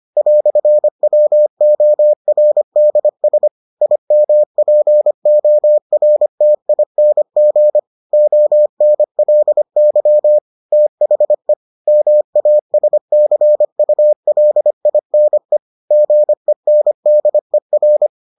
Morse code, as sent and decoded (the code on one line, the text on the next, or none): .-..-. .-- --- .-. -.. ... / .. -- .--. --- .-. - .. -. --. / --- -. .-.. -.-- / - .... / -- .- ... -.-. ..- .-.. .. -. / --. -. -.. .-.
"WORDS IMPORTING ONLY THE MASCULINE GENDER